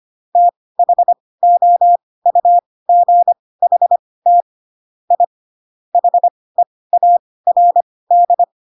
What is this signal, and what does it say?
Morse code, heard as a keyed tone, about 25 wpm, spaced as for 18 wpm, 700 Hz. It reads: THOUGHT I HEARD
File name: Pinocchio / Pinocchio ch1 12 25 ik1hge